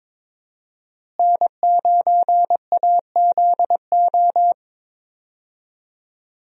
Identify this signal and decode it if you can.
Morse code, heard as a keyed tone, about 22 wpm, 700 Hz. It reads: N9AZO
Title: USA Call Signs - 5 Characters - Repeated 2x - 22wpm